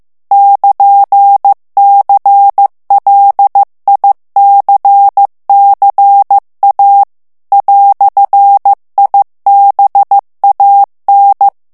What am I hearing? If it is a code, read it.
Morse code, heard as a keyed tone, about 15 wpm, 800 Hz. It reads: <KN>CLICCA "IBAN
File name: ITALIEN texte clair 1 vitesse 14 mots minute